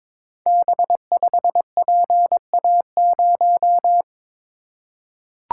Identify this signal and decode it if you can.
Morse code, heard as a keyed tone, about 22 wpm, 700 Hz. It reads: B5PA0
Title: Code Groups - 5 Letter:Numbers - 22wpm